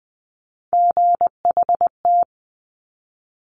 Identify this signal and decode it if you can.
Morse code, heard as a keyed tone, about 20 wpm, 700 Hz. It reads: GHT
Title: Top Three Letter Combinations in Words 20wpm